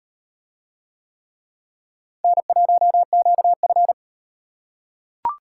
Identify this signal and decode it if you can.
Morse code, heard as a keyed tone, about 38 wpm, 700 Hz. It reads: N1QF